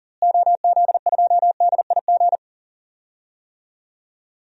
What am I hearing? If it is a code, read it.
Morse code, heard as a keyed tone, about 40 wpm, 700 Hz. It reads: OZ2DIG
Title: DXCC-100 Call Signs - 6 Characters - Repeated 2x - 40wpm